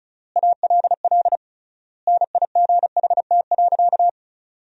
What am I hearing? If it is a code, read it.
Morse code, heard as a keyed tone, about 36 wpm, 700 Hz. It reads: ALL NIGHT.
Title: 3x Word-Spacing - Sentences from Top 300 Words - 35wpm